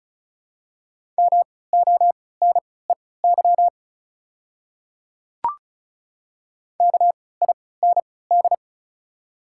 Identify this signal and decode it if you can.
Morse code, heard as a keyed tone, about 34 wpm, spaced as for 20 wpm, 700 Hz. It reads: MONEY KIND
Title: 3x Character Spacing - Top 500 Words - 35wpm